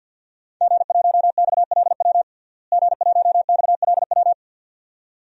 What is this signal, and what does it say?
Morse code, heard as a keyed tone, about 52 wpm, 700 Hz. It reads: G1XLW G1XLW
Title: DXCC-100 Call Signs - 5 Characters - Repeated 2x - 50wpm